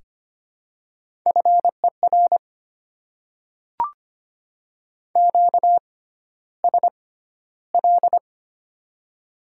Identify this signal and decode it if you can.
Morse code, heard as a keyed tone, about 25 wpm, spaced as for 13 wpm, 700 Hz. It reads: FER QSL
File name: Top 90 QSO Elements - 6x Character Spacing - 25wpm